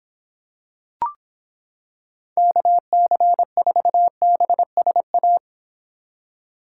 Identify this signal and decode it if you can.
Morse code, heard as a keyed tone, about 26 wpm, 700 Hz. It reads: KC4BSA